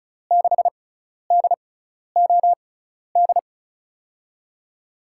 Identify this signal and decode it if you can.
Morse code, heard as a keyed tone, about 34 wpm, spaced as for 12 wpm, 700 Hz. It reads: 6DOD
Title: USA Call Signs - 6 Characters - 6x Character Spacing - 35wpm